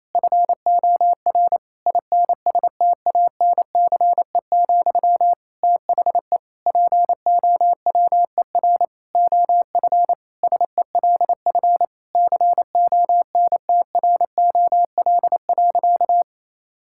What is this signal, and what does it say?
Morse code, heard as a keyed tone, about 28 wpm, 700 Hz. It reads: FOR INSTANCE, THE POWER OF SELF CONTROL.
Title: Sentences from Top 1000 Words - Review Part 5 - 28wpm